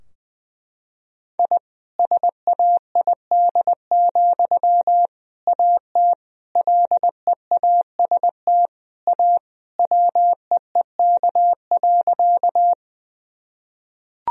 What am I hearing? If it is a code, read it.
Morse code, heard as a keyed tone, about 20 wpm, 700 Hz. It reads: I SAID, AT LEAST A WEEK.
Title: Sentences from Top 1000 Words - Review Part 2 - 20wpm